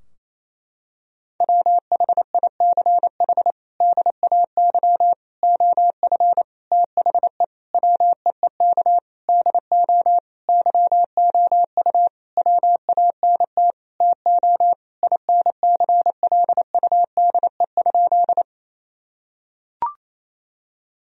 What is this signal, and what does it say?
Morse code, heard as a keyed tone, about 28 wpm, 700 Hz. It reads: WHICH DAY OF THE WEEK DO YOU WANT TO INCLUDE?
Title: Sentences from Top 1000 Words - Review Part 3 - 28wpm